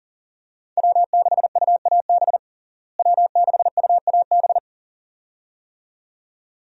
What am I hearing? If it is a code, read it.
Morse code, heard as a keyed tone, about 37 wpm, 700 Hz. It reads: W6UAB W6UAB
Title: USA Call Signs - 5 Characters - Repeated 2x - 40wpm